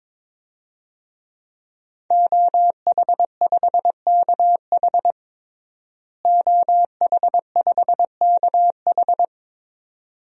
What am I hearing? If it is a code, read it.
Morse code, heard as a keyed tone, about 22 wpm, 700 Hz. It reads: OH5KH OH5KH
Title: DXCC-100 Call Signs - 5 Characters - Repeated 2x - 22wpm